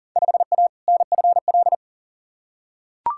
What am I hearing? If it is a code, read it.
Morse code, heard as a keyed tone, about 37 wpm, 700 Hz. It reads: 5A NFL
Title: Field Day Exchanges - 40wpm